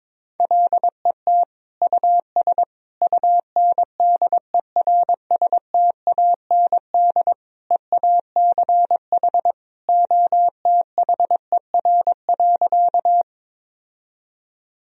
Morse code, heard as a keyed tone, about 22 wpm, 700 Hz. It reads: LET US UNDERSTAND EACH OTHER.
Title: Sentences from Top 1000 Words - Review Part 1 - 22wpm